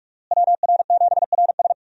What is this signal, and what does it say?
Morse code, heard as a keyed tone, about 46 wpm, 700 Hz. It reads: WR7RS
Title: USA Call Signs - 5 Characters - Repeated 3x - 45wpm